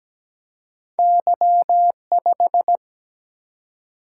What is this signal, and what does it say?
Morse code, heard as a keyed tone, about 17 wpm, 700 Hz. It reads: Y5
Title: Single-Letter Single-Number 17wpm